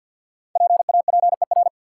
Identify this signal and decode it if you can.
Morse code, heard as a keyed tone, about 52 wpm, 700 Hz. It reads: PAPER
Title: Sentences from Top 300 Words - New Word Drill - 50wpm